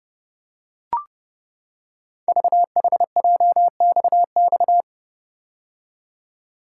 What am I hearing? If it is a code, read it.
Morse code, heard as a keyed tone, about 30 wpm, 700 Hz. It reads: VHJXX
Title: Code Groups - 5 Letters - 30wpm